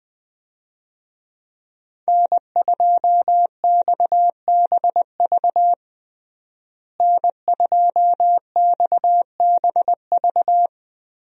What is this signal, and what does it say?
Morse code, heard as a keyed tone, about 20 wpm, 700 Hz. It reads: N2XBV N2XBV
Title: USA Call Signs - 5 Characters - Repeated 3x - 20wpm